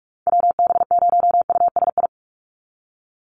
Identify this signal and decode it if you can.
Morse code, heard as a keyed tone, about 44 wpm, 700 Hz. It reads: WB0USI
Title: NA SOTA Activator Callsigns 45wpm